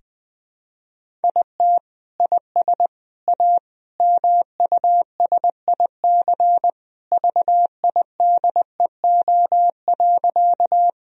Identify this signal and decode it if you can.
Morse code, heard as a keyed tone, about 21 wpm, 700 Hz. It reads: IT IS A MUSIC VIDEO.
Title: Sentences from Top 1000 Words - Review Part 4 - 20wpm